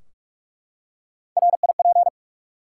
Morse code, heard as a keyed tone, about 44 wpm, 700 Hz. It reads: RIP
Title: Three Letter Words - 45wpm